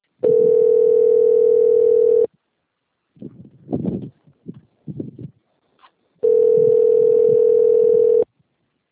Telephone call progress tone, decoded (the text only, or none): ringback tone